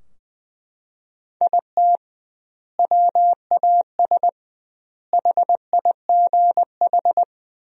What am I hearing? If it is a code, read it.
Morse code, heard as a keyed tone, about 21 wpm, 700 Hz. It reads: IT WAS HIGH
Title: 2x Word-Spacing - Sentences from Top 200 Words - 20wpm